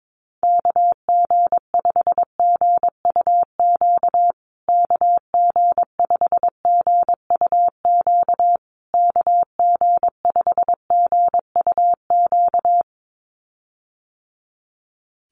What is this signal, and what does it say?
Morse code, heard as a keyed tone, about 22 wpm, 700 Hz. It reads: KG5GUQ KG5GUQ KG5GUQ
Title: USA Call Signs Repeated 3x 22wpm